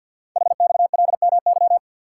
Morse code, heard as a keyed tone, about 46 wpm, 700 Hz. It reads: HXLMY
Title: Code Groups - 5 Letters - 50wpm